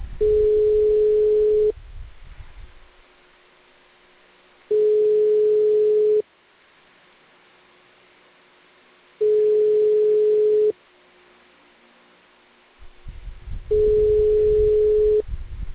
Ringback tone (425 Hz, 1.5 s on, 3 s off).